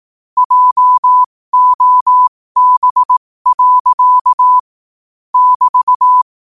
Morse code, heard as a keyed tone, about 18 wpm, spaced as for 15 wpm, 1000 Hz. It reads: JOB. <BT>